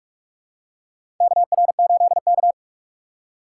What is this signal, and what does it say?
Morse code, heard as a keyed tone, about 44 wpm, 700 Hz. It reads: KR8K
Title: USA Call Signs - 4 Characters - 6x Character Spacing - 45wpm